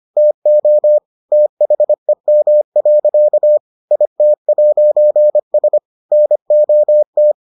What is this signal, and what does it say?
Morse code, heard as a keyed tone, about 25 wpm, 600 Hz. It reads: TO THEM. IT'S NOT